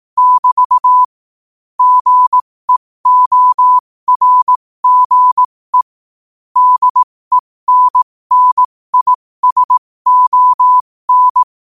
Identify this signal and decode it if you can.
Morse code, heard as a keyed tone, about 18 wpm, spaced as for 15 wpm, 1000 Hz. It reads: <BT> GEORGE DENNISON
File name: qotdfiles / 15wpm